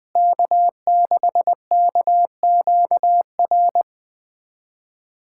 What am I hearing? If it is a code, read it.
Morse code, heard as a keyed tone, about 20 wpm, 700 Hz. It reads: K6KQR
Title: USA Callsign DE Callsign - 5 Characters - 20wpm